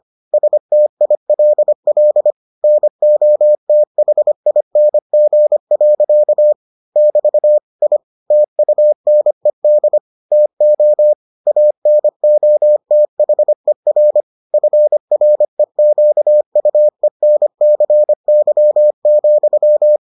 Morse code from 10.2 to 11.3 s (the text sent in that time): TO